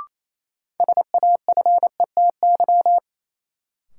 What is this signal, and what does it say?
Morse code, heard as a keyed tone, about 28 wpm, 700 Hz. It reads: SAFETY
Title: Sentences from Top 700 Words - New Word Drill - 28wpm